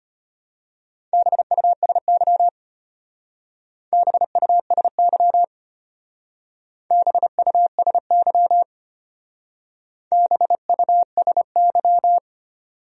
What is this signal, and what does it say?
Morse code, heard as a keyed tone, about 32 wpm, 700 Hz. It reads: BUSY BUSY BUSY BUSY